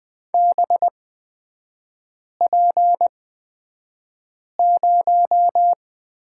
Morse code, heard as a keyed tone, about 20 wpm, spaced as for 5 wpm, 690 Hz. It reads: BP0
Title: 229 Lesson 37e - Letter x - Three Characters 20wpm